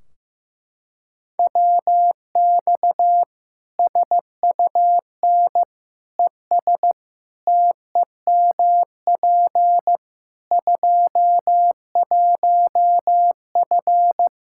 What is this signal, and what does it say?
Morse code, heard as a keyed tone, about 15 wpm, 700 Hz. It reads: WX SUN ES TEMP 21F
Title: Practice QSO - Exchange 2 - Info - 15wpm